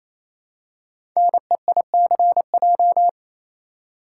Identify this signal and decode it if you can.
Morse code, heard as a keyed tone, about 28 wpm, 700 Hz. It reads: NEICJ
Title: Code Groups - 5 Letters - 28wpm